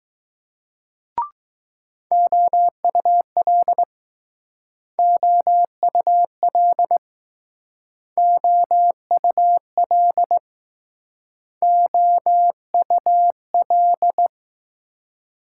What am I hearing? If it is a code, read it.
Morse code, heard as a keyed tone, about 20 wpm, 700 Hz. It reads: OUL OUL OUL OUL